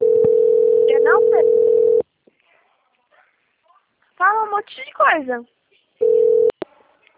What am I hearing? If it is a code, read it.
Ringback tone (North America), 440 + 480 Hz, with speech.
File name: out- - - - - .39552